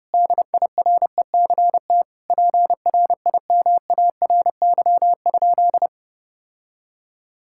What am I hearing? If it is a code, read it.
Morse code, heard as a keyed tone, about 30 wpm, 700 Hz. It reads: DIRECT PRIMARY?
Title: Sentences from Top 1000 Words - Review Part 5 - 30wpm